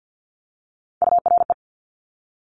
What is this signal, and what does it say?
Morse code, heard as a keyed tone, about 50 wpm, 700 Hz. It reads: URE
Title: Top Three Letter Combinations in Words 50wpm